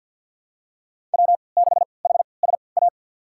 Morse code, heard as a keyed tone, about 51 wpm, spaced as for 28 wpm, 700 Hz. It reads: W6HSA
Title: USA Call Signs - 5 Characters - 3x Character Spacing - 50wpm